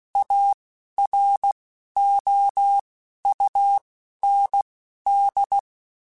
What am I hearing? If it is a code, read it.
Morse code, heard as a keyed tone, about 16 wpm, spaced as for 11 wpm, 800 Hz. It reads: AROUND
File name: QSOs / QSOs-mp3s / WD0BGP 2@10